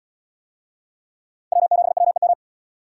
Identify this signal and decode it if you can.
Morse code, heard as a keyed tone, about 73 wpm, 700 Hz. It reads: W7WEM